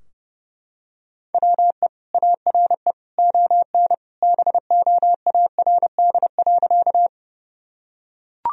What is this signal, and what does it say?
Morse code, heard as a keyed tone, about 31 wpm, 700 Hz. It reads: WE ARE ON BOARD.
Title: Sentences from Top 1000 Words - Review Part 4 - 30wpm